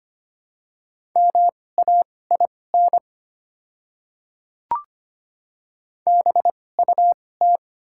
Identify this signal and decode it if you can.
Morse code, heard as a keyed tone, about 25 wpm, spaced as for 18 wpm, 700 Hz. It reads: MAIN BUT
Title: 2x Character Spacing - Top 300 Words - 25wpm